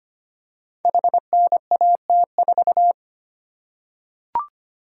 Morse code, heard as a keyed tone, about 25 wpm, 700 Hz. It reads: HNAT4
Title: Code Groups - 5 Letter:Numbers - 25wpm